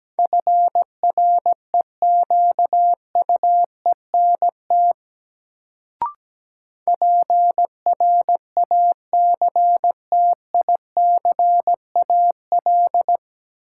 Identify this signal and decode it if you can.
Morse code, heard as a keyed tone, about 17 wpm, 700 Hz. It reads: FREQUENT PRACTICAL
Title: Sentences from Top 1400 Words - New Word Drill - 17wpm